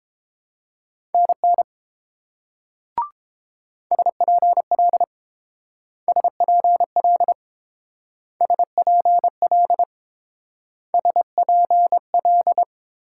Morse code, heard as a keyed tone, about 29 wpm, 700 Hz. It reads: NN SPL SPL SPL SPL